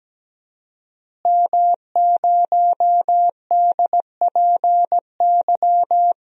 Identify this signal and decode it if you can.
Morse code, heard as a keyed tone, about 17 wpm, 700 Hz. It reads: M0DPY
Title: DXCC-100 Call Signs - 5 Characters - Repeated 3x - 17wpm